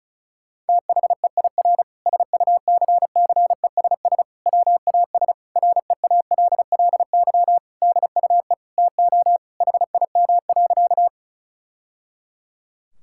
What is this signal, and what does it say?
Morse code, heard as a keyed tone, about 36 wpm, 700 Hz. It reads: THEIR SUCCESS WAS REALLY DUE TO HIM.
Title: Sentences from Top 1000 Words - Review Part 6 - 35wpm